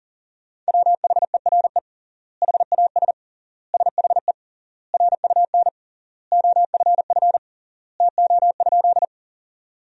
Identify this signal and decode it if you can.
Morse code, heard as a keyed tone, about 40 wpm, 700 Hz. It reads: WHERE HAS SHE RUN OFF TO?
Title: 3x Word-Spacing - Sentences from Top 300 Words - 40wpm